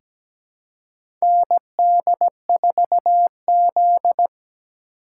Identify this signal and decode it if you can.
Morse code, heard as a keyed tone, about 17 wpm, 700 Hz. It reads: ND4Z
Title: USA Call Signs - 4 Characters - 4x Character Spacing - 17wpm